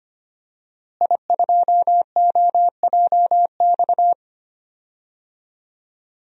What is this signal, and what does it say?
Morse code, heard as a keyed tone, about 25 wpm, 700 Hz. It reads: I2OJX
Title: DXCC-100 Call Signs - 5 Characters - Repeated 3x - 25wpm